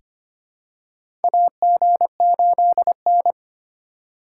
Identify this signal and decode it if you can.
Morse code, heard as a keyed tone, about 25 wpm, 700 Hz. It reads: AG8N
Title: USA Call Signs - 2x Character Spacing - 25wpm